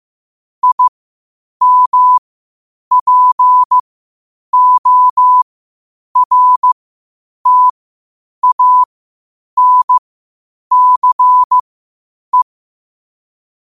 Morse code, heard as a keyed tone, about 15 wpm, spaced as for 8 wpm, 1000 Hz. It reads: IMPORTANCE
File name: qotdfiles / 8wpm